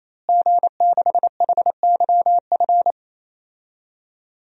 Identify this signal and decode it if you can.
Morse code, heard as a keyed tone, about 27 wpm, 700 Hz. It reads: G6HYF
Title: DXCC-100 Call Signs - 5 Characters - Repeated 3x - 28wpm